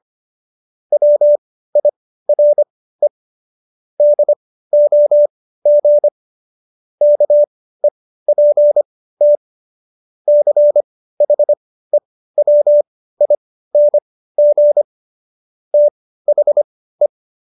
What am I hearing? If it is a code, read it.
Morse code, heard as a keyed tone, about 25 wpm, spaced as for 15 wpm, 590 Hz. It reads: WIRE DOG KEPT CHEWING THE